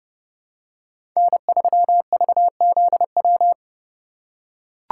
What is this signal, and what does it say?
Morse code, heard as a keyed tone, about 30 wpm, 700 Hz. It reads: N3VZW
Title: USA Call Signs - 5 Characters - 3x Character Spacing - 30wpm